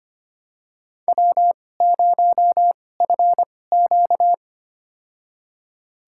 Morse code, heard as a keyed tone, about 25 wpm, spaced as for 18 wpm, 700 Hz. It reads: W0FQ
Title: USA Call Signs - 2x Character Spacing - 25wpm